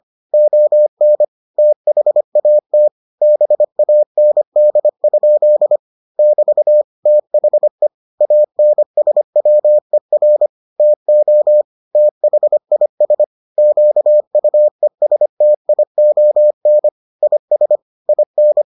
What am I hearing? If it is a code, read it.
Morse code, heard as a keyed tone, about 25 wpm, 600 Hz. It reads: ON THAT BAND? <BT> THE ANSWER TO THIS QUESTION IS IN